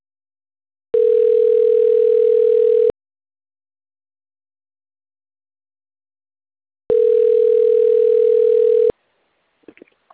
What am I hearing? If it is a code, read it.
Ringback tone (North America), 440 + 480 Hz.